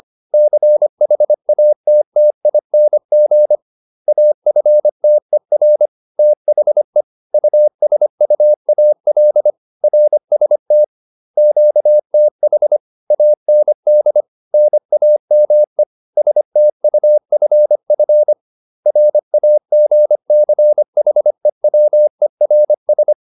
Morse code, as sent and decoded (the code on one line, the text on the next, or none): -.-. .... .- - - .. -. --. / .- ..-. - . .-. / - .... / ..- ... ..- .- .-.. / .-. ... - / --.- - .... / .- -. -.. / -. .- -- . / ... - ..- ..-. ..-. / .-. .- --. -.-. .... .-- . .-. ...
CHATTING AFTER THE USUAL RST QTH AND NAME STUFF RAGCHEWERS